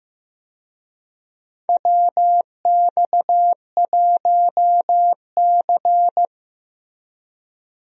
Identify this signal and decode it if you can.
Morse code, heard as a keyed tone, about 15 wpm, 700 Hz. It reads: WX1C
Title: USA Call Signs - 4 Characters - Repeated 2x - 15wpm